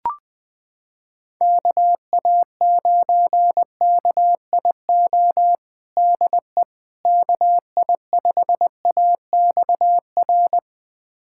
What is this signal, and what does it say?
Morse code, heard as a keyed tone, about 21 wpm, 700 Hz. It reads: KA9KIO DE KI5AXR
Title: USA Callsign DE Callsign - 6 Characters - 20wpm